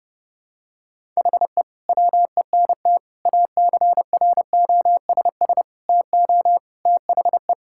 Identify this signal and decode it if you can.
Morse code, heard as a keyed tone, about 30 wpm, 700 Hz. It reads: HE WENT ACROSS TO THE
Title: Sentences from Top 1000 Words - Review Part 5 - 30wpm